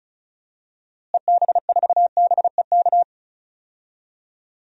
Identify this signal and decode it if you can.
Morse code, heard as a keyed tone, about 36 wpm, 700 Hz. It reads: EB4BEK
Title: DXCC-100 Call Signs - 6 Characters - Repeated 3x - 35wpm